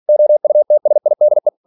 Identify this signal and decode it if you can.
Morse code, heard as a keyed tone, about 46 wpm, 600 Hz. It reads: OUTSIDE
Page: Never 47wpm